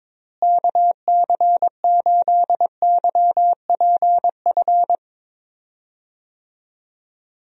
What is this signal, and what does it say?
Morse code, heard as a keyed tone, about 22 wpm, 700 Hz. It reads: KC8YPF
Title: USA Call Signs - 6 Characters - Repeated 3x - 22wpm